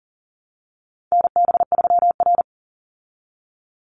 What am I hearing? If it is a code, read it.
Morse code, heard as a keyed tone, about 40 wpm, 700 Hz. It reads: NB3R
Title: Call Signs Repeated 2x - 40wpm